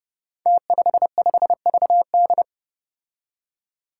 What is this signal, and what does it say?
Morse code, heard as a keyed tone, about 31 wpm, 700 Hz. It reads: T55VD
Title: Five Challenging Letter-Number Combinations 30wpm